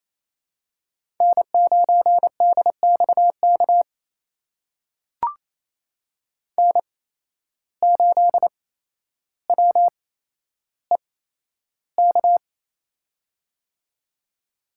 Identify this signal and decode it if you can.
Morse code, heard as a keyed tone, about 28 wpm, spaced as for 12 wpm, 700 Hz. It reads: N9DXK N8WEK
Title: USA Call Signs - 5 Characters - 8x Character Spacing - 28wpm